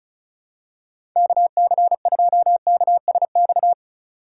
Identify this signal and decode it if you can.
Morse code, heard as a keyed tone, about 36 wpm, 700 Hz. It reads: KC2KSX
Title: USA Call Signs - 3x Character Spacing - 35wpm